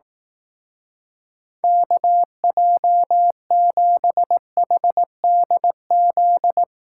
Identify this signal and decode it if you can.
Morse code, heard as a keyed tone, about 19 wpm, 700 Hz. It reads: KJ7HDZ
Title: Speed Racing - USA Call Signs 15wpm